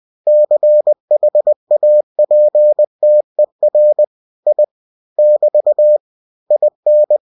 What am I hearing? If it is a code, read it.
Morse code, heard as a keyed tone, about 20 wpm, 600 Hz. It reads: CHAPTER I <BT> IN
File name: cw-books / winnie-the-pooh / winnie-20wpm-page-0000